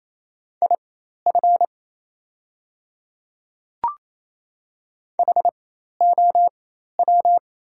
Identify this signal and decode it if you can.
Morse code, heard as a keyed tone, about 28 wpm, spaced as for 13 wpm, 700 Hz. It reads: IF HOW
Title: 4x Character Spacing - Top 100 Words - 28wpm